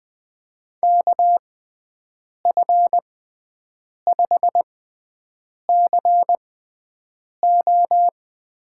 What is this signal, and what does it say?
Morse code, heard as a keyed tone, about 20 wpm, spaced as for 7 wpm, 700 Hz. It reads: KF5CO
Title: USA Call Signs - 6 Characters - 6x Character Spacing - 20wpm